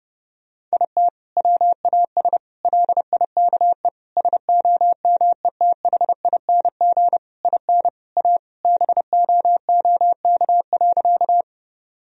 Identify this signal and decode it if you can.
Morse code, heard as a keyed tone, about 30 wpm, 700 Hz. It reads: IT WAS LIKE SOMETHING IN A BOOK.
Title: Sentences from Top 1000 Words - Review Part 1 - 30wpm